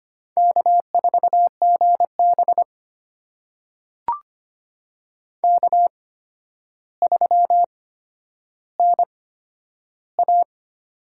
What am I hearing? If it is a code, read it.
Morse code, heard as a keyed tone, about 25 wpm, spaced as for 11 wpm, 700 Hz. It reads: K4GB K3NA
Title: USA Call Signs - 8x Character Spacing - 25wpm